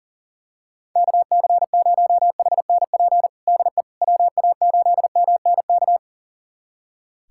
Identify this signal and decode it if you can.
Morse code, heard as a keyed tone, about 37 wpm, 700 Hz. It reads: KC0HNP DE WA8MNK
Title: USA Callsign DE Callsign 40wpm